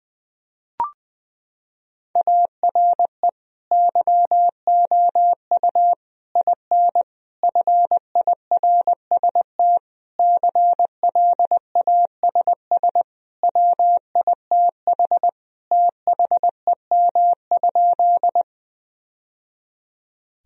Morse code, heard as a keyed tone, about 21 wpm, 700 Hz. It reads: ARE YOU IN FIRST CLASS WITH THEM?
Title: Sentences from Top 1000 Words - Review Part 2 - 20wpm